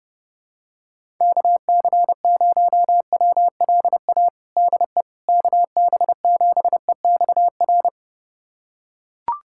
Morse code, heard as a keyed tone, about 30 wpm, 700 Hz. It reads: KC0WLA DE KB7EXR
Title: USA Callsign DE Callsign 30wpm